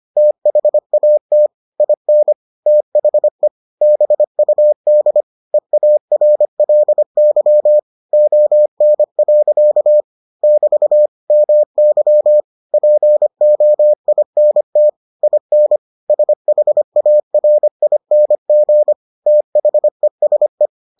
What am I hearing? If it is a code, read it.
Morse code, heard as a keyed tone, about 25 wpm, 600 Hz. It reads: THAT IN THE BUD EARLY ON. <BT> MY POINT IN SHARING THESE